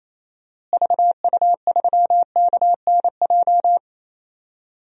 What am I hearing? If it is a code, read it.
Morse code, heard as a keyed tone, about 28 wpm, 700 Hz. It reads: VU3KNJ
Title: DXCC-100 Call Signs - 6 Characters - Repeated 2x - 28wpm